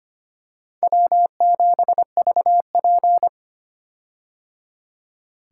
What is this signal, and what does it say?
Morse code, heard as a keyed tone, about 25 wpm, 700 Hz. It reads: W7VP
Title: K1USN SST QSO Practice - 25wpm